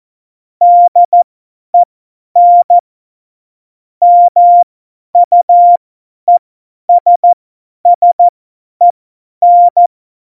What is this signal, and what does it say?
Morse code, heard as a keyed tone, about 14 wpm, spaced as for 10 wpm, 700 Hz. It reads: DEN MUESSEN
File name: Morsekurs / Witze / Witz 05. Juni 2018, 50 F70